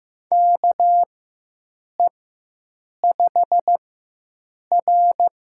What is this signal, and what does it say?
Morse code, heard as a keyed tone, about 15 wpm, spaced as for 7 wpm, 700 Hz. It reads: KE5R
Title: USA Call Signs - 4x Character Spacing - 15wpm